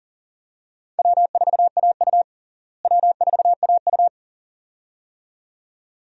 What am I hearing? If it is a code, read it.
Morse code, heard as a keyed tone, about 37 wpm, 700 Hz. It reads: W4AU W4AU
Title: USA Call Signs - 4 Characters - Repeated 2x - 40wpm